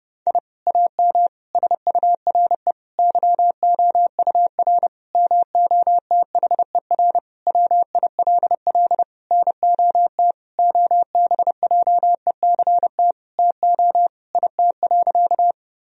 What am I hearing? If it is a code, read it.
Morse code, heard as a keyed tone, about 29 wpm, 700 Hz. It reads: I AM SURE YOUR MOTHER WILL NOT OBJECT TO IT.
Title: Sentences from Top 1000 Words - Review Part 3 - 30wpm